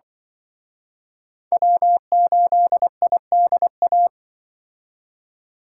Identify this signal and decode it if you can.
Morse code, heard as a keyed tone, about 25 wpm, 700 Hz. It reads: W8IDA
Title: Speed Racing - USA 1x3 Call Signs 20wpm